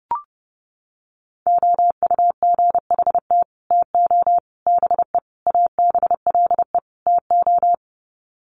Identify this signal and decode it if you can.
Morse code, heard as a keyed tone, about 30 wpm, 700 Hz. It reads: OUGHT TO BE ABLE TO
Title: Sets of 5 Words Encore2 30wpm